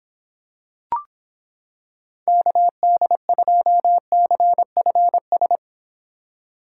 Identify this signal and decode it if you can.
Morse code, heard as a keyed tone, about 26 wpm, 700 Hz. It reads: KD2CFS